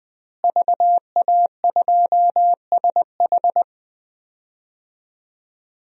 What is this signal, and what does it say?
Morse code, heard as a keyed tone, about 20 wpm, 700 Hz. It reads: VA2SH
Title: DXCC-100 Call Signs - 5 Characters - Repeated 3x - 20wpm